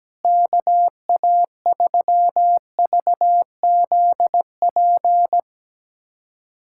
Morse code, heard as a keyed tone, about 17 wpm, 700 Hz. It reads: KA3VZP
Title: USA Callsign DE Callsign - 6 Characters - 17wpm